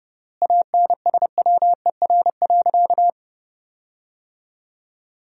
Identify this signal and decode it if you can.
Morse code, heard as a keyed tone, about 31 wpm, 700 Hz. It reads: ANSWER.
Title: 3x Word-Spacing - Sentences from Top 300 Words - 30wpm